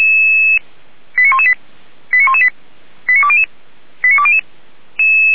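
Five-tone selective call (ZVEI), heard as quick runs of five tones, tones of about 70 ms. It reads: 89118 89118 89200 89200
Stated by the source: ZVEI five-tone selective call received by SDR.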